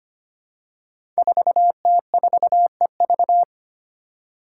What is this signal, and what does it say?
Morse code, heard as a keyed tone, about 25 wpm, 700 Hz. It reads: 4T4EV
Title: Five Challenging Letter-Number Combinations 25wpm